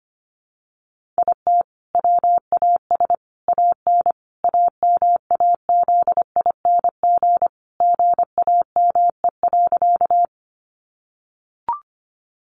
Morse code, heard as a keyed tone, about 25 wpm, 700 Hz. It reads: IT WAS AN AMAZING GAME.
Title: Sentences from Top 1500 Words 25wpm